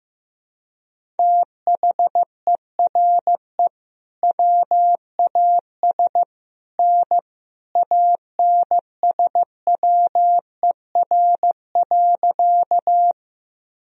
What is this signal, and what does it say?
Morse code, heard as a keyed tone, about 15 wpm, 700 Hz. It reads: THERE WAS N ANSWER.
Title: Sentences from Top 300 Words and Abbreviations 15wpm